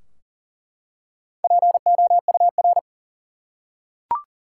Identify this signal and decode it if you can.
Morse code, heard as a keyed tone, about 40 wpm, 700 Hz. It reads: POUR
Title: Four Letter Words - 40wpm